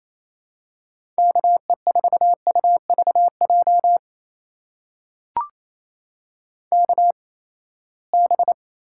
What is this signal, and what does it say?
Morse code, heard as a keyed tone, about 28 wpm, spaced as for 19 wpm, 700 Hz. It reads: KE4UVJ KB
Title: USA Call Signs - 6 Characters - 8x Character Spacing - 28wpm